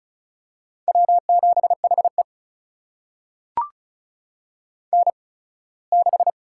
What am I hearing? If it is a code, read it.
Morse code, heard as a keyed tone, about 36 wpm, spaced as for 21 wpm, 700 Hz. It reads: W7HE N6
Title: USA Call Signs - 4 Characters - 8x Character Spacing - 35wpm